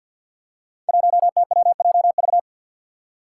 Morse code, heard as a keyed tone, about 53 wpm, 700 Hz. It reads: 1TWJV
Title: Five Challenging Letter-Number Combinations 50wpm